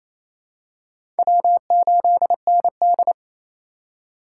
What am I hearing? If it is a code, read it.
Morse code, heard as a keyed tone, about 28 wpm, 700 Hz. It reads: W8ND
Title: USA Call Signs - 4x Character Spacing - 28wpm